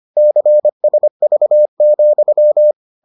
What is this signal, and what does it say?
Morse code, heard as a keyed tone, about 25 wpm, 600 Hz. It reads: CSV,